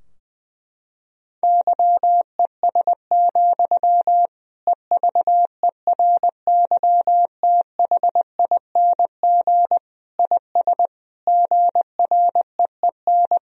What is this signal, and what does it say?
Morse code, heard as a keyed tone, about 20 wpm, 700 Hz. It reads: YES, EVERYTHING IS GREEN
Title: Sentences from Top 1000 Words - Review Part 2 - 20wpm